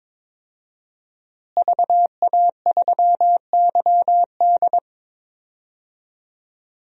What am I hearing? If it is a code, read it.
Morse code, heard as a keyed tone, about 22 wpm, 700 Hz. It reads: VA3YD
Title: DXCC-100 Call Signs - 5 Characters - Repeated 2x - 22wpm